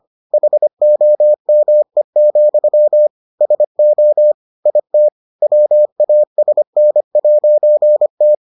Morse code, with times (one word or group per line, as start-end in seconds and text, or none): 0.3-3.1 HOME,
3.4-4.3 SO
4.7-5.1 IT
5.4-8.3 WASN'T